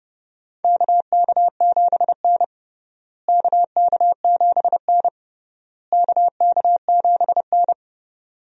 Morse code, heard as a keyed tone, about 30 wpm, 700 Hz. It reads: KK7N KK7N KK7N
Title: USA Call Signs - 4 Characters - Repeated 3x - 30wpm